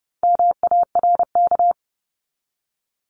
Morse code, heard as a keyed tone, about 30 wpm, 700 Hz. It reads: MARK
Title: Top 500 Words Repeated 30wpm